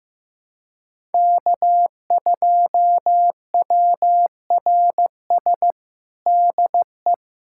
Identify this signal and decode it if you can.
Morse code, heard as a keyed tone, about 16 wpm, 700 Hz. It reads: K2WRS DE
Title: USA Callsign DE Callsign 15wpm